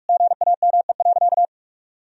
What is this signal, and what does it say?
Morse code, heard as a keyed tone, about 46 wpm, 700 Hz. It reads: GAME.
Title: Sentences from Top 1000 Words - Review Part 3 - 45wpm